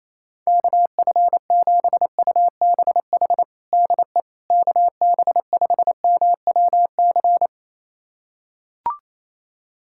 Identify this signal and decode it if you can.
Morse code, heard as a keyed tone, about 27 wpm, 700 Hz. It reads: KF7UBH DE KB5MWC
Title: USA Callsign DE Callsign - 6 Characters - 28wpm